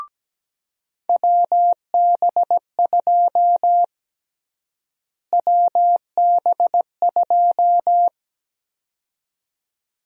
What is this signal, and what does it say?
Morse code, heard as a keyed tone, about 17 wpm, 700 Hz. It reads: WB2 WB2
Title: USA Call Sign Prefixes - Repeated 2x - 17wpm